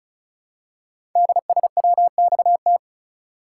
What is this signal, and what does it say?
Morse code, heard as a keyed tone, about 36 wpm, 700 Hz. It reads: DSWXT
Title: Code Groups - 5 Letters - 35wpm